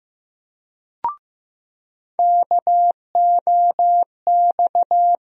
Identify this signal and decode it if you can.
Morse code, heard as a keyed tone, about 15 wpm, 700 Hz. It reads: KOX
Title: USA Call Sign Postfixes - 3 Letters 15wpm